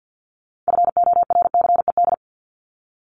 Morse code, heard as a keyed tone, about 50 wpm, 700 Hz. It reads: FORCED
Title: ED Suffix - Top 100 Words with Base Words 50wpm